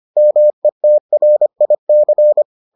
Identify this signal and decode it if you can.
Morse code, heard as a keyed tone, about 25 wpm, 600 Hz. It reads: METRIC